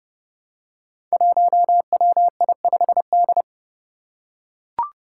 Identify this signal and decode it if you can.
Morse code, heard as a keyed tone, about 30 wpm, 700 Hz. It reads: 1WI5D
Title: Five Challenging Letter-Number Combinations 30wpm